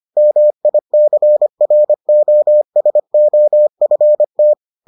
Morse code, heard as a keyed tone, about 25 wpm, 600 Hz. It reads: MICROSOFT